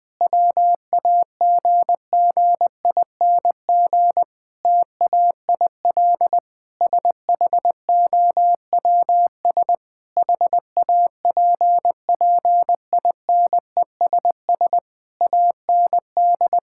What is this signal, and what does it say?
Morse code, heard as a keyed tone, about 21 wpm, 700 Hz. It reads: WAGGING TAIL SHOWS HAPPINESS AND
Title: Difficult Sentences with Hints - 20wpm